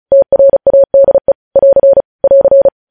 Morse code, heard as a keyed tone, about 36 wpm, 550 Hz. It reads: TRADE <AR> <AR>
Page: News headlines in Morse code at 35 wpm